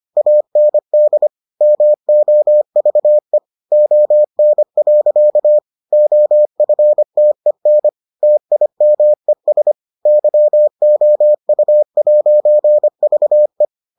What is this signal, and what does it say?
Morse code, heard as a keyed tone, about 25 wpm, 600 Hz. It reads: AND MOVE ON. OFTEN TIMES YOU'VE